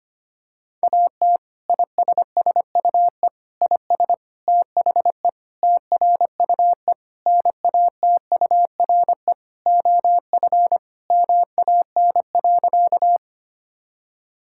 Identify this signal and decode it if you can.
Morse code, heard as a keyed tone, about 25 wpm, 700 Hz. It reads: AT ISSUE IS THE TRUE NATURE OF MAN.
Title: Sentences from Top 1000 Words - Review Part 6 - 25wpm